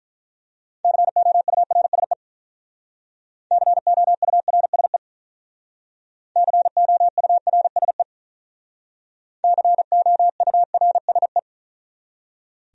Morse code, heard as a keyed tone, about 45 wpm, 700 Hz. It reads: COURSE COURSE COURSE COURSE